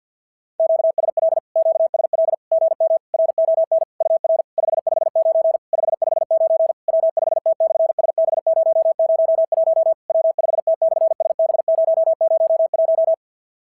Morse code, heard as a keyed tone, about 50 wpm, 650 Hz. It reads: QSL QSL GM RON UR 559 559 W5T/SB001 W5T/SB001
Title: SOTA QSO Practice 50wpm